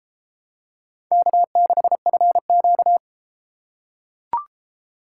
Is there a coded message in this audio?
Yes: Morse code from 1.1 to 3.0 s, heard as a keyed tone, about 31 wpm, 700 Hz. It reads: K6FQ